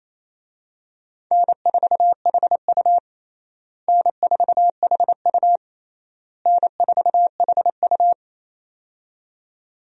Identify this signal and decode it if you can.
Morse code, heard as a keyed tone, about 28 wpm, 700 Hz. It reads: N4HU N4HU N4HU
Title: USA Call Signs - 4 Characters - Repeated 3x - 28wpm